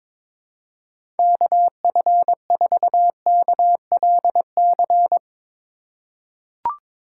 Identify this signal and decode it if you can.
Morse code, heard as a keyed tone, about 22 wpm, 700 Hz. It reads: KF4KLC
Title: USA Call Signs - 6 Characters - 3x Character Spacing - 22wpm